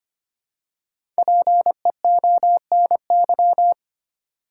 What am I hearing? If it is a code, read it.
Morse code, heard as a keyed tone, about 25 wpm, 700 Hz. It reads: PEONY
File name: practice / 25wpm / Difficult Words with Hints - 25wpm